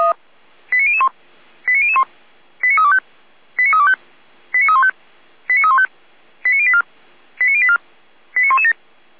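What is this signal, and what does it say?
Five-tone selective call (ZVEI), heard as quick runs of five tones, tones of about 70 ms. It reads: 89001 89001 89325 89325 89315 89315 89084 89084 89117
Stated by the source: Received on SDR, ZVEI five-tone call 89099, recorded by feuerwehr-orlamuende (fire brigade Orlamünde).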